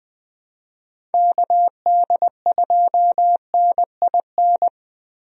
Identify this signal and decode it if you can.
Morse code, heard as a keyed tone, about 20 wpm, 700 Hz. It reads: KD2NIN
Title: USA Call Signs - 6 Characters - Repeated 3x - 20wpm